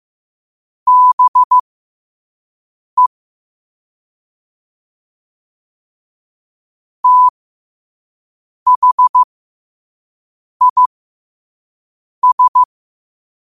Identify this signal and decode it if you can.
Morse code, heard as a keyed tone, about 15 wpm, spaced as for 5 wpm, 1000 Hz. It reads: BE THIS
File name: qotdfiles / 5wpm